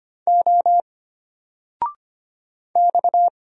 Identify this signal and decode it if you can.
Morse code, heard as a keyed tone, about 25 wpm, 700 Hz. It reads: O X
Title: Single Letter-Number 25wpm